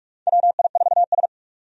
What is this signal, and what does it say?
Morse code, heard as a keyed tone, about 44 wpm, 700 Hz. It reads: WI4S
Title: USA Call Signs - 4 Characters - 45wpm